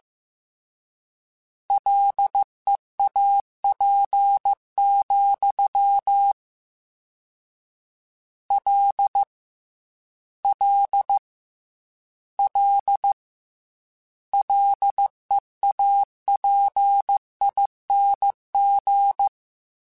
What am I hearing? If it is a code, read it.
Morse code, heard as a keyed tone, about 15 wpm, 800 Hz. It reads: LEAP, L L L LEAPING
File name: Combinations / Combinations-mp3s / Combinations@13